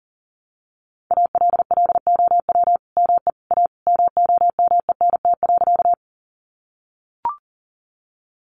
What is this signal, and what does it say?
Morse code, heard as a keyed tone, about 40 wpm, 700 Hz. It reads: ALLOW ME A MOMENT.
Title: Sentences from Top 1000 Words 40wpm